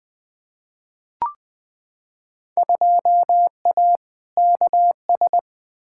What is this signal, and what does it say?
Morse code, heard as a keyed tone, about 20 wpm, 700 Hz. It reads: 2A KS
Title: Field Day Exchanges - 20wpm